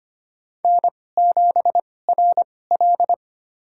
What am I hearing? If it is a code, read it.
Morse code, heard as a keyed tone, about 25 wpm, spaced as for 18 wpm, 700 Hz. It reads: N7RL
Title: USA Call Signs - 2x Character Spacing - 25wpm